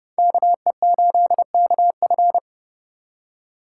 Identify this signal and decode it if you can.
Morse code, heard as a keyed tone, about 28 wpm, 700 Hz. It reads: KE8KF